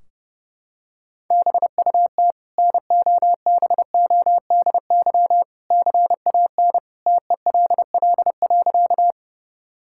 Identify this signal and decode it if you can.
Morse code, heard as a keyed tone, about 30 wpm, 700 Hz. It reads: BUT NOBODY CAN TELL.
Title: Sentences from Top 1600 Words 30wpm